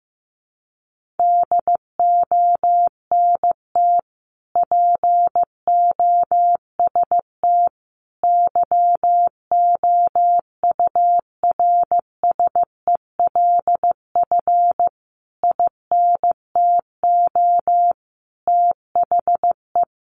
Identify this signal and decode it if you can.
Morse code, heard as a keyed tone, about 15 wpm, 700 Hz. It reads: DONT POST YOURSELF INTO THE
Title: Sentences from Top 1000 Words 15wpm